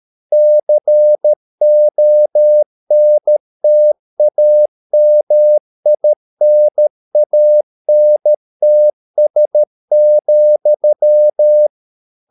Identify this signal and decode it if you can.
Morse code, heard as a keyed tone, about 13 wpm, 600 Hz. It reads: CONTAMINANTS,